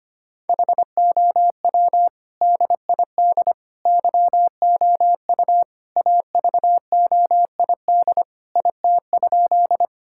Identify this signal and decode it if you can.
Morse code, heard as a keyed tone, about 25 wpm, 700 Hz. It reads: HOW DID YOU AVOID IT?
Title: Sentences from Top 1000 Words - Review Part 6 - 25wpm